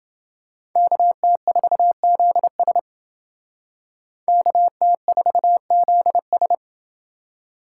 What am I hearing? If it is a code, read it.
Morse code, heard as a keyed tone, about 28 wpm, 700 Hz. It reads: KT4ZS KT4ZS